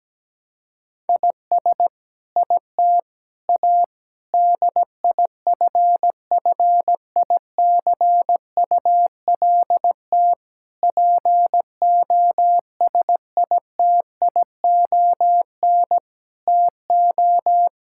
Morse code, heard as a keyed tone, about 17 wpm, 700 Hz. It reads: IS IT A DIFFICULT POSITION TO
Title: Sentences from Top 1000 Words - Review Part 5 - 17wpm